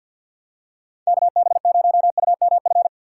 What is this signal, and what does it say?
Morse code, heard as a keyed tone, about 47 wpm, 700 Hz. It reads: KB0UMF
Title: USA Call Signs - 6 Characters - 4x Character Spacing - 50wpm